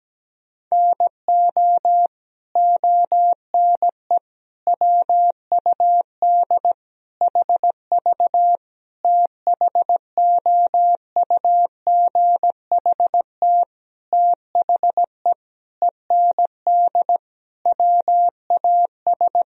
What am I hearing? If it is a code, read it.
Morse code, heard as a keyed tone, about 17 wpm, 700 Hz. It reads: NO ONE WUD HV THOUGHT THE END WAS
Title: Sentences from Top 300 Words and Abbreviations 17wpm